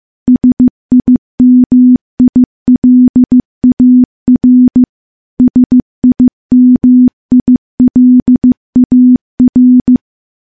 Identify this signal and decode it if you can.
Morse code, heard as a keyed tone, about 15 wpm, 250 Hz. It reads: SIMILAR SIMILAR
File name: morse-training-15WPM